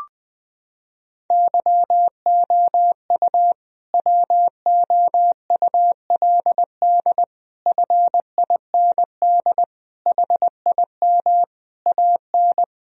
Morse code, heard as a keyed tone, about 20 wpm, 700 Hz. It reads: YOU WOULD FIND HIM AN
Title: Sentences from Top 1000 Words - Review Part 1 - 20wpm